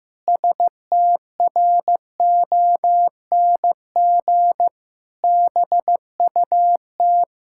Morse code, heard as a keyed tone, about 15 wpm, 700 Hz. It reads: STRONG BUT
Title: Pangram Challenge v2 - 15wpm